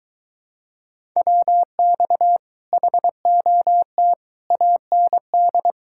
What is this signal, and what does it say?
Morse code, heard as a keyed tone, about 23 wpm, 700 Hz. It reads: WX HOT AND